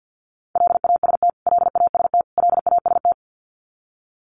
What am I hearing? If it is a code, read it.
Morse code, heard as a keyed tone, about 48 wpm, 700 Hz. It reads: LAST LAST LAST
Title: Top 300 Words Repeated 50wpm